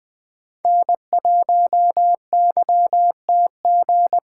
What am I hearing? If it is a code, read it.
Morse code, heard as a keyed tone, about 20 wpm, 700 Hz. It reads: N1YTG
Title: USA Call Signs - 5 Characters - 2x Character Spacing - 20wpm